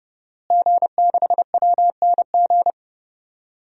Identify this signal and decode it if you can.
Morse code, heard as a keyed tone, about 31 wpm, 700 Hz. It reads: G6WNG
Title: DXCC-100 Call Signs - 5 Characters - Repeated 3x - 30wpm